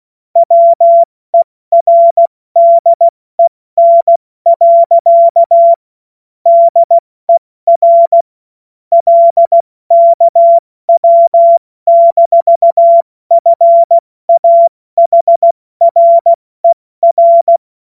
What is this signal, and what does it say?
Morse code, heard as a keyed tone, about 16 wpm, 680 Hz. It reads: WERDEN. DER LKW-FAHRER
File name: Morsekurs / Witze / Witz 20. September 2022 70 F80